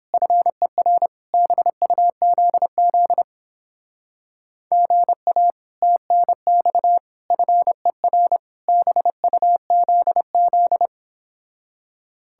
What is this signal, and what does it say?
Morse code, heard as a keyed tone, about 29 wpm, 700 Hz. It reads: FER BUZZ GA TNX FER BUZZ